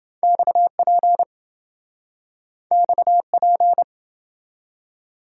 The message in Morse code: XP XP

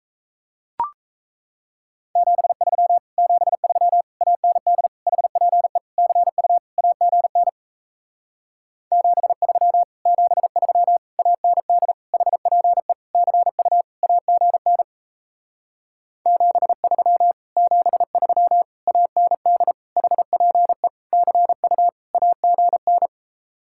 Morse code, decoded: 73 73 AND HPE CU AGN 73 73 AND HPE CU AGN 73 73 AND HPE CU AGN

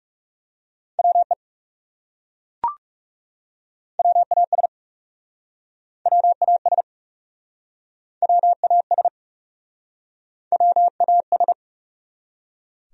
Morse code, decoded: WE WAS WAS WAS WAS

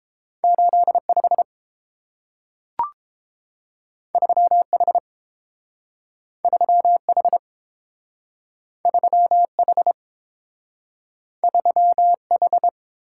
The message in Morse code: 85 3H 3H 3H 3H

